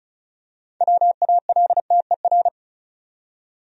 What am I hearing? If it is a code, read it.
Morse code, heard as a keyed tone, about 36 wpm, 700 Hz. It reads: WALTER
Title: USA Common Names - 35wpm